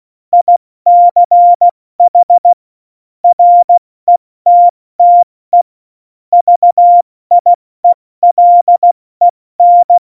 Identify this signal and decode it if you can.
Morse code, heard as a keyed tone, about 16 wpm, 700 Hz. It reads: ICH RETTE VIELEN
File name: Morsekurs / Witze / Witz 19. Januar 2021, 70 F80